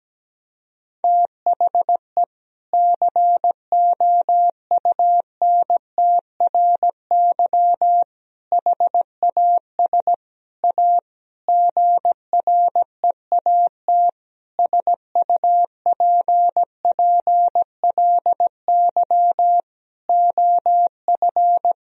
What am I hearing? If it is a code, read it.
Morse code, heard as a keyed tone, about 17 wpm, 700 Hz. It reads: THE COUNTRY HAS A GREAT SUPPLY OF
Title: Sentences from Top 1000 Words - Review Part 4 - 17wpm